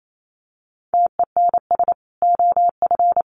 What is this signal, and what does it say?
Morse code, heard as a keyed tone, about 28 wpm, 700 Hz. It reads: TENS OF
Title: Sets of 5 Words Encore2 28wpm